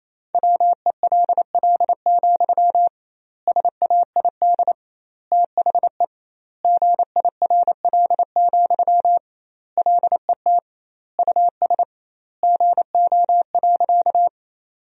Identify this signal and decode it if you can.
Morse code, heard as a keyed tone, about 28 wpm, 700 Hz. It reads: WELL, SAID THE GIRL, LET US GO.
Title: 2x Word-Spacing - Sentences from Top 400 Words - 28wpm